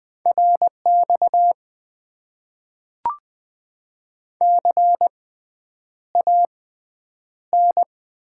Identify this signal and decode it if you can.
Morse code, heard as a keyed tone, about 20 wpm, spaced as for 9 wpm, 700 Hz. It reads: RX CAN
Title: Top 90 QSO Elements - 6x Character Spacing - 20wpm